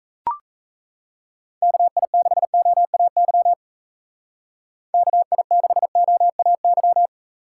Morse code, heard as a keyed tone, about 35 wpm, 700 Hz. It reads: KI6OAY KI6OAY
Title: Speed Racing - USA 2x3 Call Signs 28wpm